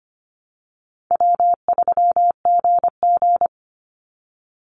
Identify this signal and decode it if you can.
Morse code, heard as a keyed tone, about 25 wpm, 700 Hz. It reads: W3GG
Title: Call Signs Repeated 3x - 25wpm